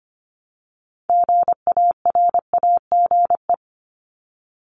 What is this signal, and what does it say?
Morse code, heard as a keyed tone, about 24 wpm, 700 Hz. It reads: GARAGE
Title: Multi-letter Phonemes Repeated 3x 25wpm